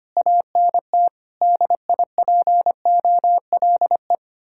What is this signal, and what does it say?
Morse code, heard as a keyed tone, about 32 wpm, 700 Hz. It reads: ANT DIPOLE